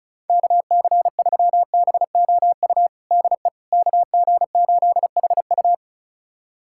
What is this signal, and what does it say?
Morse code, heard as a keyed tone, about 34 wpm, 700 Hz. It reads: KC3BOU DE KG8HU
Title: USA Callsign DE Callsign 35wpm